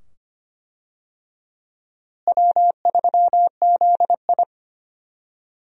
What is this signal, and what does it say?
Morse code, heard as a keyed tone, about 25 wpm, 700 Hz. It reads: W3ZI
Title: USA Call Signs - 4 Characters - Repeated 2x - 25wpm